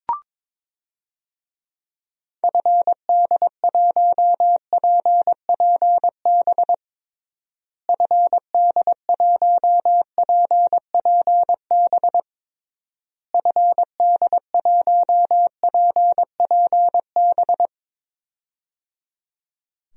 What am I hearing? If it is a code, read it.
Morse code, heard as a keyed tone, about 22 wpm, 700 Hz. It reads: FD1PPB FD1PPB FD1PPB
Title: DXCC-100 Call Signs - 6 Characters - Repeated 3x - 22wpm